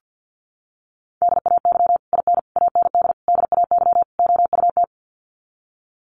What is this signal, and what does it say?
Morse code, heard as a keyed tone, about 50 wpm, 700 Hz. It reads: DAY IN AND DAY OUT
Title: Sets of 5 Words Encore2 50wpm